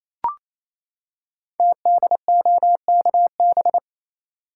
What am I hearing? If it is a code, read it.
Morse code, heard as a keyed tone, about 29 wpm, 700 Hz. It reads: TDOKB
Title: Code Groups - 5 Letters - 28wpm